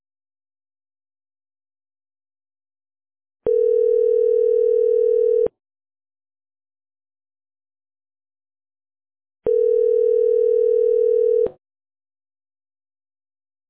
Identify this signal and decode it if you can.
Ringback tone (North America), 440 + 480 Hz.